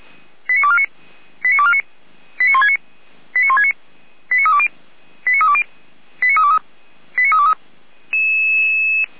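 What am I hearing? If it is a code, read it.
Five-tone selective call (ZVEI), heard as quick runs of five tones, tones of about 80 ms. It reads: 89249 89249 89169 89169 89320 89320 89323 89323
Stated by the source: Five-tone ZVEI selective call received by SDR, from audio.f85360.feuerwehr-orlamuende.org (Feuerwehr Orlamünde fire brigade).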